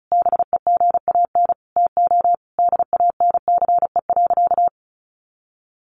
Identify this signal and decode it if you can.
Morse code, heard as a keyed tone, about 34 wpm, 700 Hz. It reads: BEGAN TO DANCE.
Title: Sentences from Top 900 Words 35wpm